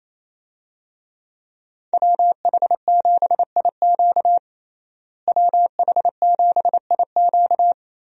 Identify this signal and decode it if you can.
Morse code, heard as a keyed tone, about 28 wpm, 700 Hz. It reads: WH7IQ WH7IQ
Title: USA Call Signs - 5 Characters - Repeated 3x - 28wpm